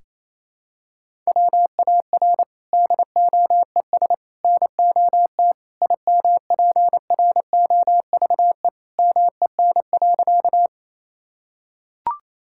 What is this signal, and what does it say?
Morse code, heard as a keyed tone, about 28 wpm, 700 Hz. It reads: WAR DOES NOT IMPROVE MEN.
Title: Sentences from Top 1000 Words - Review Part 5 - 28wpm